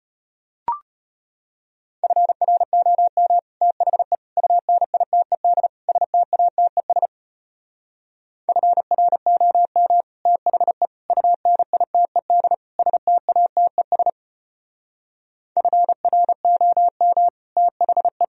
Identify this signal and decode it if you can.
Morse code, heard as a keyed tone, about 33 wpm, 700 Hz. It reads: FROM THE UNITED STATES FROM THE UNITED STATES FROM THE